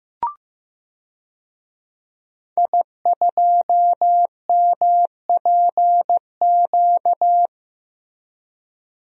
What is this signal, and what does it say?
Morse code, heard as a keyed tone, about 15 wpm, 700 Hz. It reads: I2MPQ
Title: DXCC-100 Call Signs - 5 Characters - 15wpm